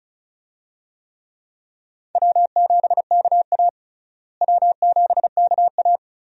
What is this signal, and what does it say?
Morse code, heard as a keyed tone, about 36 wpm, 700 Hz. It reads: W7KA W7KA
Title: USA Call Signs - 4 Characters - Repeated 2x - 35wpm